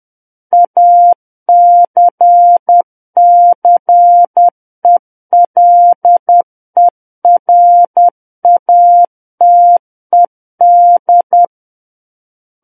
Morse code, heard as a keyed tone, about 10 wpm, 700 Hz. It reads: ACCELERATED
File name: Book One / 01-Chapter 1 - 10 WPM